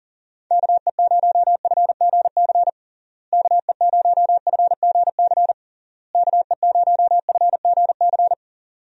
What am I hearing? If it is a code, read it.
Morse code, heard as a keyed tone, about 40 wpm, 700 Hz. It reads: KE0FGC KE0FGC KE0FGC
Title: USA Call Signs - 6 Characters - Repeated 3x - 40wpm